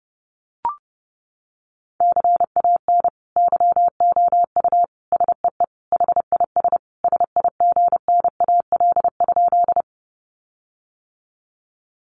Morse code, heard as a keyed tone, about 30 wpm, 700 Hz. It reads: CAN YOU SEE HIS SIGNAL?
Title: Sentences from Top 1400 Words 30wpm